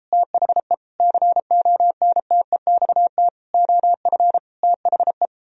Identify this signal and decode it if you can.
Morse code, heard as a keyed tone, about 36 wpm, 700 Hz. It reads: THE CONTEXT OF THE